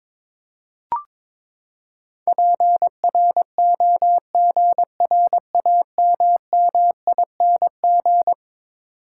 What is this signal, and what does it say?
Morse code, heard as a keyed tone, about 22 wpm, 700 Hz. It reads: PROGRAMMING
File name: practice / 22wpm / Eleven Letter Words - 22wpm